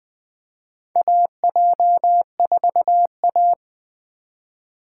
Morse code, heard as a keyed tone, about 21 wpm, 700 Hz. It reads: AJ4A
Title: USA Call Signs - 4 Characters - 4x Character Spacing - 20wpm